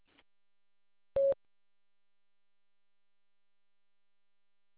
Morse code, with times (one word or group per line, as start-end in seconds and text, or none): none